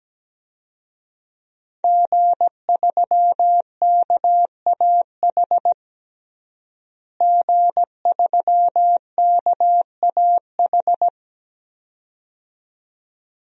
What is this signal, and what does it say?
Morse code, heard as a keyed tone, about 17 wpm, 700 Hz. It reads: G3KAH G3KAH
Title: DXCC-100 Call Signs - 5 Characters - Repeated 2x - 17wpm